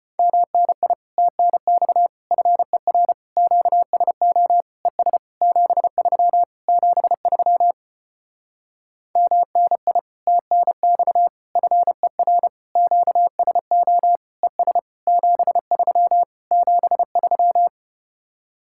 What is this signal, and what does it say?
Morse code, heard as a keyed tone, about 33 wpm, 700 Hz. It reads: MNI TNX FER QSO ES 73 73 MNI TNX FER QSO ES 73 73